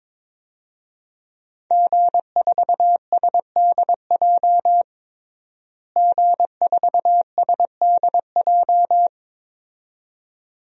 Morse code, heard as a keyed tone, about 22 wpm, 700 Hz. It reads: G4SDJ G4SDJ
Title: DXCC-100 Call Signs - 5 Characters - Repeated 2x - 22wpm